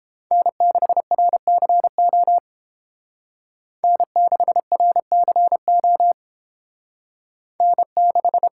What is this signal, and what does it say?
Morse code, heard as a keyed tone, about 31 wpm, 700 Hz. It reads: N6RCO N6RCO N6